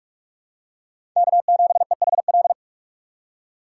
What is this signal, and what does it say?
Morse code, heard as a keyed tone, about 46 wpm, 700 Hz. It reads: K7EHL
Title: USA Call Signs - 5 Characters - 4x Character Spacing - 45wpm